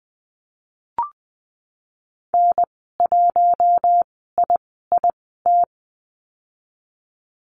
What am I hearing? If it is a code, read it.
Morse code, heard as a keyed tone, about 19 wpm, spaced as for 14 wpm, 700 Hz. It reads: N1IIT
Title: 048 Lesson 9g - Number 4 - Call Signs 20wpm